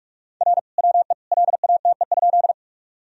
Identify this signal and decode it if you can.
Morse code, heard as a keyed tone, about 46 wpm, 700 Hz. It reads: R WE LATE?
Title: Sentences from Top 300 Words and Abbreviations 45wpm